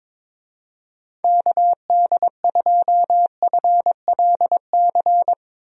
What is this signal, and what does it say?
Morse code, heard as a keyed tone, about 22 wpm, 700 Hz. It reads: KD2FLC
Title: USA Call Signs - 6 Characters - Repeated 2x - 22wpm